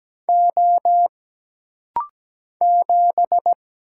Morse code, heard as a keyed tone, about 17 wpm, 700 Hz. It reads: O 7
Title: Single Letter-Number Alt - Rapid-Fire 17wpm